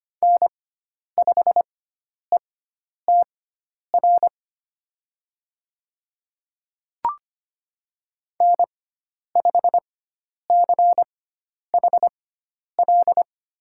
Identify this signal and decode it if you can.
Morse code, heard as a keyed tone, about 25 wpm, spaced as for 10 wpm, 700 Hz. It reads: N5ETR N5CHL
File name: Lesson 16g - Letter c - Call Signs 25wpm